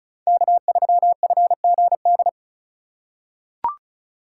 Morse code, heard as a keyed tone, about 34 wpm, 700 Hz. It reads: K3FGD
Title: USA Call Signs - 5 Characters - 4x Character Spacing - 35wpm